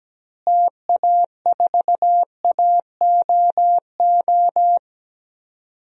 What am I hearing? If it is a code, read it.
Morse code, heard as a keyed tone, about 17 wpm, 700 Hz. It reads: TA4AOO
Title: DXCC-100 Call Signs - 6 Characters - Repeated 3x - 17wpm